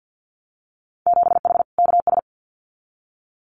Morse code, heard as a keyed tone, about 50 wpm, 700 Hz. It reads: 7H KS
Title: Winter Field Day Exchanges Encore 50wpm